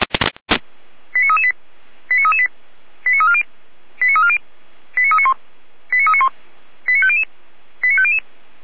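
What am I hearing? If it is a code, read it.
Five-tone selective call (ZVEI), heard as quick runs of five tones, tones of about 80 ms. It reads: 89338 89338 89340 89340 89391 89391 89500 89500
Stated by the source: ZVEI five-tone call received by SDR.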